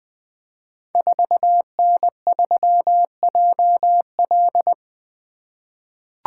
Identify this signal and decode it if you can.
Morse code, heard as a keyed tone, about 20 wpm, 700 Hz. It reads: 4N3JL
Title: Code Groups - 5 Letter:Numbers - 20wpm